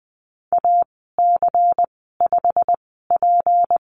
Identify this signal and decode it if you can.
Morse code, heard as a keyed tone, about 20 wpm, spaced as for 15 wpm, 700 Hz. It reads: AC5P
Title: 193 Lesson 31g - Letter v - Call Signs 20wpm